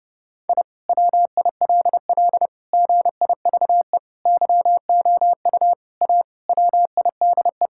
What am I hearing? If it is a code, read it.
Morse code, heard as a keyed tone, about 30 wpm, 700 Hz. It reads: I WILL GIVE YOU A WIDE
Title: Sentences from Top 1000 Words - Review Part 6 - 30wpm